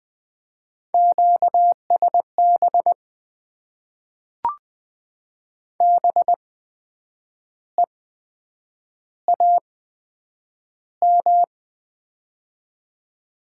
Morse code, heard as a keyed tone, about 20 wpm, spaced as for 8 wpm, 700 Hz. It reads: QSB BEAM
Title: Top 90 QSO Elements - 8x Character Spacing - 20wpm